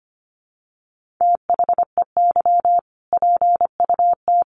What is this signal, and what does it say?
Morse code, heard as a keyed tone, about 25 wpm, 700 Hz. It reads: THEY PUT
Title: Sentences from Top 1500 Words 25wpm